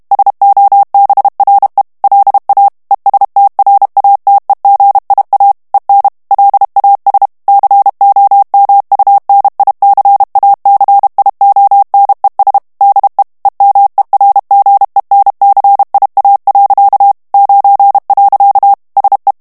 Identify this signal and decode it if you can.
Morse code, heard as a keyed tone, about 32 wpm, 800 Hz. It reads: SOBRE LA ESTRATEGIA EN LAS COMUNICACIONES DE EMERGENCIA. 9. SE
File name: morse mp3 / ESPAGNOL TEXTE EN CLAIR 1 VITESSE 30 MOTS MINUTES